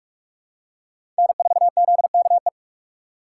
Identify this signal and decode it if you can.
Morse code, heard as a keyed tone, about 48 wpm, 700 Hz. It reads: N4ZKE
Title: USA Call Signs - 5 Characters - 6x Character Spacing - 45wpm